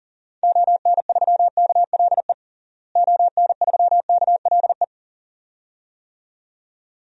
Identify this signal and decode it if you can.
Morse code, heard as a keyed tone, about 40 wpm, 700 Hz. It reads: ON3KLE ON3KLE
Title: DXCC-100 Call Signs - 6 Characters - Repeated 2x - 40wpm